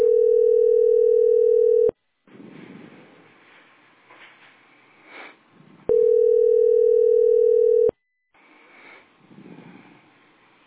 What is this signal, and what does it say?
Ringback tone (North America), 440 + 480 Hz.